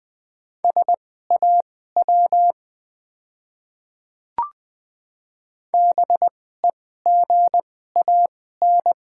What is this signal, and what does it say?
Morse code, heard as a keyed tone, about 20 wpm, spaced as for 14 wpm, 700 Hz. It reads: SAW BEGAN
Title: 2x Character Spacing - Top 300 Words - 20wpm